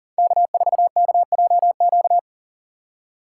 Morse code, heard as a keyed tone, about 40 wpm, 700 Hz. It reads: K4KJQ
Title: USA Call Signs - 5 Characters - 3x Character Spacing - 40wpm